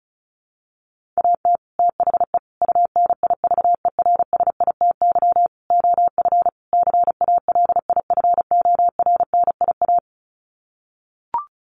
Morse code, heard as a keyed tone, about 34 wpm, 700 Hz. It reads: AT THE UNIVERSITY OF CALIFORNIA
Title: Sets of 5 Words Encore2 35wpm